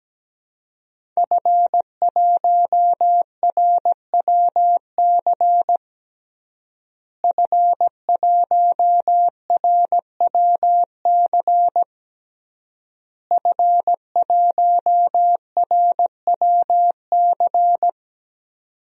Morse code, heard as a keyed tone, about 17 wpm, 700 Hz. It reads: F1RWC F1RWC F1RWC
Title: DXCC-100 Call Signs - 5 Characters - Repeated 3x - 17wpm